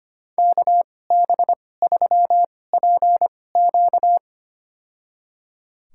Morse code, heard as a keyed tone, about 25 wpm, spaced as for 18 wpm, 700 Hz. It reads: KB3PQ
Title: USA Call Signs - 5 Characters - 2x Character Spacing - 25wpm